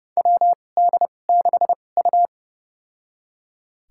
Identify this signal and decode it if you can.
Morse code, heard as a keyed tone, about 30 wpm, spaced as for 22 wpm, 700 Hz. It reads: WD6U
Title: USA Call Signs - 4 Characters - 2x Character Spacing - 30wpm